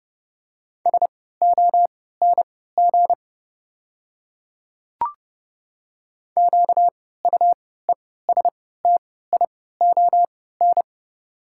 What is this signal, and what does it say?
Morse code, heard as a keyed tone, about 30 wpm, spaced as for 17 wpm, 690 Hz. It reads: SONG QUESTION
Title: 3x Character Spacing - Top 500 Words - 30wpm